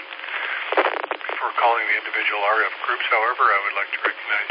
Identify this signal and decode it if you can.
Emergency Alert System (SAME) tones, heard as harsh buzzing bursts, with speech. It is heard from 0.2 to 4.0 s.